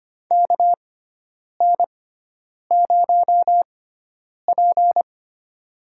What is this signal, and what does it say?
Morse code, heard as a keyed tone, about 25 wpm, spaced as for 9 wpm, 700 Hz. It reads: KN0P
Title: USA Call Signs - 4 Characters - 6x Character Spacing - 25wpm